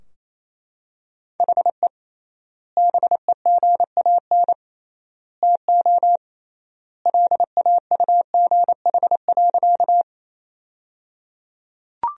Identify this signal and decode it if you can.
Morse code, heard as a keyed tone, about 28 wpm, 700 Hz. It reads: HE BEGAN TO LAUGH.
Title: 3x Word-Spacing - Sentences from Top 500 Words - 28wpm